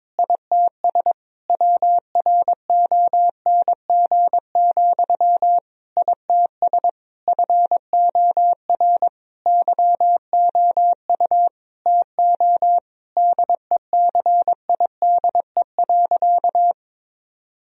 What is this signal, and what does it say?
Morse code, heard as a keyed tone, about 22 wpm, 700 Hz. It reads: ITS WRONG, ITS FOR YOU TO DECIDE.
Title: Sentences from Top 1000 Words - Review Part 3 - 22wpm